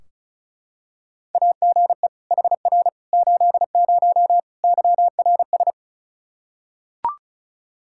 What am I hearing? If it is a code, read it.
Morse code, heard as a keyed tone, about 36 wpm, 700 Hz. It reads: AGE HR 80 YRS
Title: K1USN SST QSO Practice 35wpm